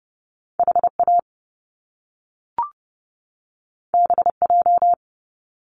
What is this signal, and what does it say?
Morse code, heard as a keyed tone, about 30 wpm, 700 Hz. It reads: HA BJ
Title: Top Two Letter Combinations in Words 30wpm